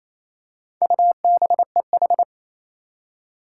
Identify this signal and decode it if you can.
Morse code, heard as a keyed tone, about 28 wpm, 700 Hz. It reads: UBEH
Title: Four Challenging Letter-Number Combinations 28wpm